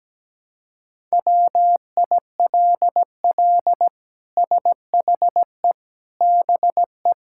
Morse code, heard as a keyed tone, about 17 wpm, 700 Hz. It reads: WILL SHE BE
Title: Sentences from Top 1000 Words - Review Part 6 - 17wpm